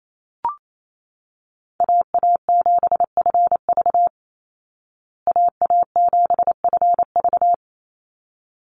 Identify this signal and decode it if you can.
Morse code, heard as a keyed tone, about 28 wpm, 700 Hz. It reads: AA7FV AA7FV
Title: Call Signs Repeated 3x - 28wpm